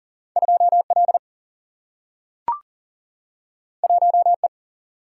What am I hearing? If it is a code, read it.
Morse code, heard as a keyed tone, about 40 wpm, 700 Hz. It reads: 2 L 1 E
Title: Single-Letter Single-Number 40wpm